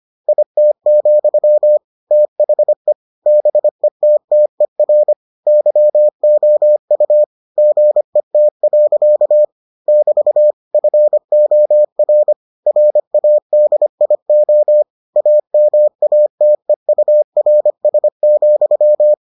Morse code, heard as a keyed tone, about 25 wpm, 600 Hz. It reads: IT, THE BETTER YOU GET. <BT> FOR RADIO AMATEURS,